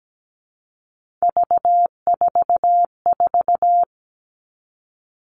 Morse code, heard as a keyed tone, about 17 wpm, 700 Hz. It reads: V44
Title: Three Challenging Letter-Number Combinations 17wpm